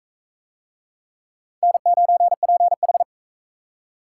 Morse code, heard as a keyed tone, about 44 wpm, 700 Hz. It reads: N9PH